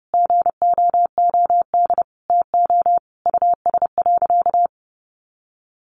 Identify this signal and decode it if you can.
Morse code, heard as a keyed tone, about 30 wpm, 700 Hz. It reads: GOOD TO US.
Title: Sentences from Top 200 Words 30wpm